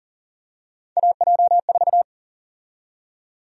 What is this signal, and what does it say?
Morse code, heard as a keyed tone, about 40 wpm, 700 Hz. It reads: AJ4
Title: USA Call Sign Prefixes - 40wpm